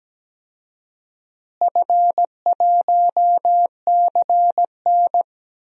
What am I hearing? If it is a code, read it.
Morse code, heard as a keyed tone, about 17 wpm, 700 Hz. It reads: F1CN
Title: DXCC-100 Call Signs - 4 Characters - Repeated 3x - 17wpm